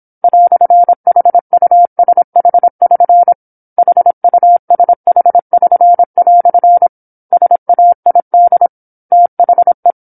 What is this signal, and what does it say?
Morse code, heard as a keyed tone, about 27 wpm, 700 Hz. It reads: "HUSH<SN> HUSH<SN>" SAID THE
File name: 08-Chapter-08 26 WPM